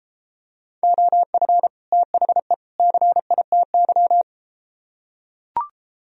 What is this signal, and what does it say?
Morse code, heard as a keyed tone, about 32 wpm, 700 Hz. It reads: OF THE CITY